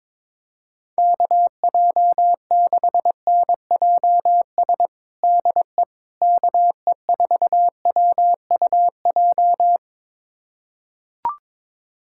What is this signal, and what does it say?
Morse code, heard as a keyed tone, about 22 wpm, 700 Hz. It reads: KJ6NJS DE KE4WUJ
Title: USA Callsign DE Callsign 22wpm